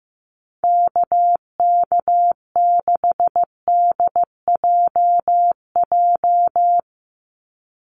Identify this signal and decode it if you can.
Morse code, heard as a keyed tone, about 15 wpm, 700 Hz. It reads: KK6DJJ
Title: USA Call Signs 15wpm